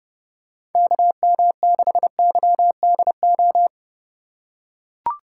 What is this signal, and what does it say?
Morse code, heard as a keyed tone, about 30 wpm, 700 Hz. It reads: KM6YDO
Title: USA Call Signs - 6 Characters - 4x Character Spacing - 30wpm